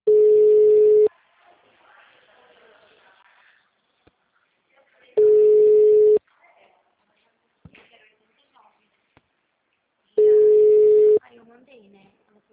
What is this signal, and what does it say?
Ringback tone (Europe, 425 Hz), with speech.
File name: out- -2009- - - .536